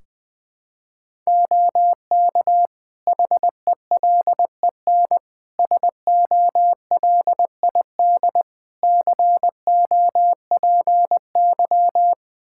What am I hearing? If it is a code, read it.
Morse code, heard as a keyed tone, about 20 wpm, 700 Hz. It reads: OK HELEN SOLID COPY
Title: Practice QSO - Exchange 3 - More Info - 20wpm